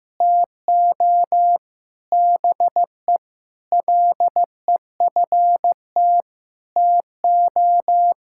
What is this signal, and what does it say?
Morse code, heard as a keyed tone, about 15 wpm, 700 Hz. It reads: TO BE LEFT TO
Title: Sentences from Top 1000 Words - Review Part 5 - 15wpm